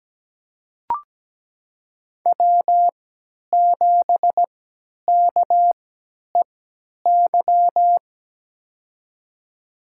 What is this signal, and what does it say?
Morse code, heard as a keyed tone, about 17 wpm, spaced as for 10 wpm, 700 Hz. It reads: W7KEY
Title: USA Call Signs - 5 Characters - 3x Character Spacing - 17wpm